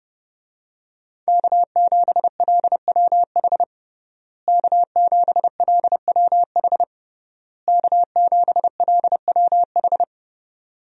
Morse code, heard as a keyed tone, about 31 wpm, 700 Hz. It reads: K7LWH K7LWH K7LWH
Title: USA Call Signs - 5 Characters - Repeated 3x - 30wpm